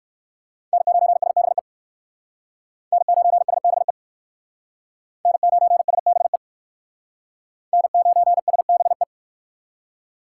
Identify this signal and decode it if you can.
Morse code, heard as a keyed tone, about 56 wpm, 690 Hz. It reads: N9SBE N9SBE N9SBE N9SBE